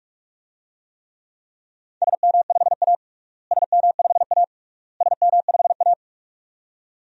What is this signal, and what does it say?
Morse code, heard as a keyed tone, about 41 wpm, 700 Hz. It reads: SM5A SM5A SM5A
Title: DXCC-100 Call Signs - 4 Characters - Repeated 3x - 45wpm